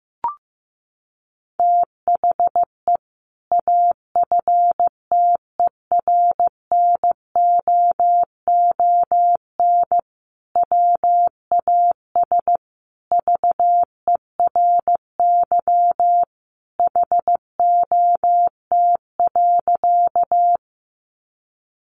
Morse code, heard as a keyed tone, about 15 wpm, 700 Hz. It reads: THE AFTERNOON WAS VERY HOT.
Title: Sentences from Top 1500 Words 15wpm